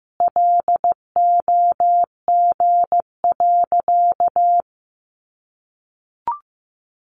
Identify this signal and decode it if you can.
Morse code, heard as a keyed tone, about 15 wpm, 700 Hz. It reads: LOG.
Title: Sentences from Top 1500 Words 15wpm